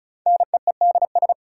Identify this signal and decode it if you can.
Morse code, heard as a keyed tone, about 36 wpm, 700 Hz. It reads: NEEDS
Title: Sentences from Top 1000 Words - Review Part 3 - New Word Drill - 35wpm